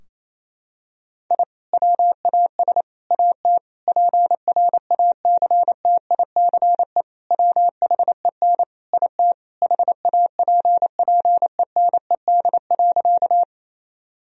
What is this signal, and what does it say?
Morse code, heard as a keyed tone, about 28 wpm, 700 Hz. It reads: I WAS AT PRACTICE WHEN IT HAPPENED.
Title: Sentences from Top 1000 Words - Review Part 4 - 28wpm